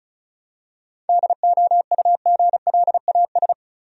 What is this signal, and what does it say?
Morse code, heard as a keyed tone, about 34 wpm, 700 Hz. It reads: DOUGLAS
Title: USA Common Names - 35wpm